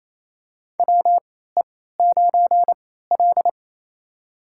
Morse code, heard as a keyed tone, about 28 wpm, spaced as for 16 wpm, 700 Hz. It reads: WE9L
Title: USA Call Signs - 4 Characters - 3x Character Spacing - 28wpm